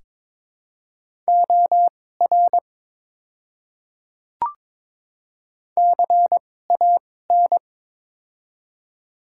Morse code, heard as a keyed tone, about 23 wpm, spaced as for 16 wpm, 700 Hz. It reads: OR CAN
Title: 2x Character Spacing - Top 100 Words - 22wpm